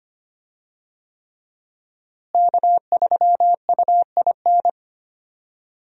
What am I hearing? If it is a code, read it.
Morse code, heard as a keyed tone, about 25 wpm, 700 Hz. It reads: K3UIN
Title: USA Call Signs - 5 Characters - Repeated 3x - 25wpm